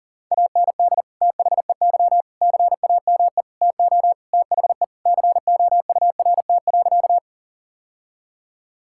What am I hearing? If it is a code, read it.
Morse code, heard as a keyed tone, about 40 wpm, 700 Hz. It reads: AND THEY CAME TO THE COURT.
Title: Sentences from Top 1600 Words 40wpm